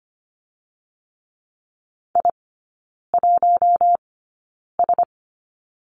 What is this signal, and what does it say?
Morse code, heard as a keyed tone, about 25 wpm, spaced as for 9 wpm, 690 Hz. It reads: I1S
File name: Lesson 15e - Number 5 - Three Characters 25wpm